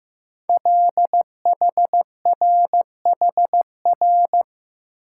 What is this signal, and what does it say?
Morse code, heard as a keyed tone, about 15 wpm, 700 Hz. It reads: LHRHR
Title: Code Groups - 5 Letter:Numbers - 15wpm